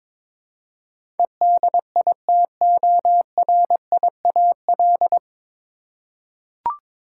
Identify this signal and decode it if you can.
Morse code, heard as a keyed tone, about 22 wpm, 700 Hz. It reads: EDITORIAL
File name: 9 Letter Words - First 3 Letters Given - 22wpm